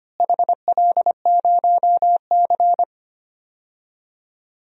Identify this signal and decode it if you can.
Morse code, heard as a keyed tone, about 25 wpm, 700 Hz. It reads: HL0C
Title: DXCC-100 Call Signs - 4 Characters - Repeated 3x - 25wpm